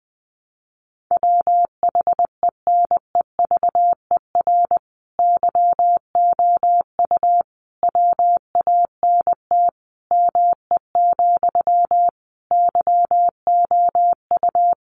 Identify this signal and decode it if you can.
Morse code, heard as a keyed tone, about 20 wpm, 700 Hz. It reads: WHENEVER YOU WANT ME, YOU
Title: Sentences from Top 1200 Words 20wpm